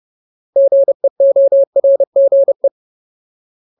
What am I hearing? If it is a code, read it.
Morse code, heard as a keyed tone, about 30 wpm, 550 Hz. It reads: GEORGE
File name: Ultimate Word List / 30WPM Ultimate